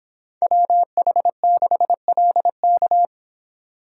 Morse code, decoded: WH6LK

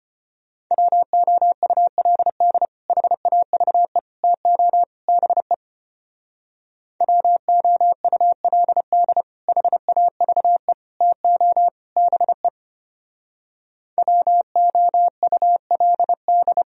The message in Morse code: WOULD HAVE TO BE WOULD HAVE TO BE WOULD

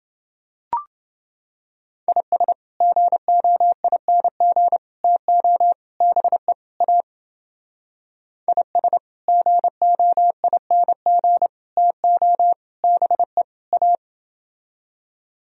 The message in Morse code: IS GOING TO BE A IS GOING TO BE A

